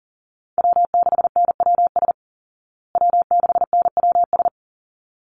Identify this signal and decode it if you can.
Morse code, heard as a keyed tone, about 40 wpm, 700 Hz. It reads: W6NWS W6NWS
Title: Call Signs Repeated 3x - 40wpm